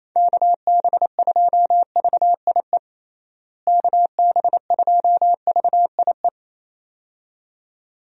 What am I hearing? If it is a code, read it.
Morse code, heard as a keyed tone, about 28 wpm, 700 Hz. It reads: KB2VIE KB2VIE
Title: USA Call Signs - 6 Characters - Repeated 2x - 28wpm